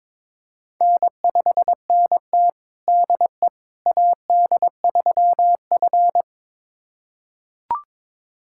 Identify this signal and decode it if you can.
Morse code, heard as a keyed tone, about 22 wpm, 700 Hz. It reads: N5NT DE AD3F
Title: USA Callsign DE Callsign - 4 Characters - 22wpm